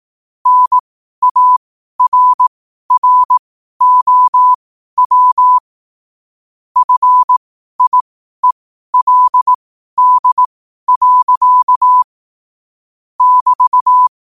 Morse code, heard as a keyed tone, about 18 wpm, spaced as for 12 wpm, 1000 Hz. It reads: NARROW FIELD. <BT>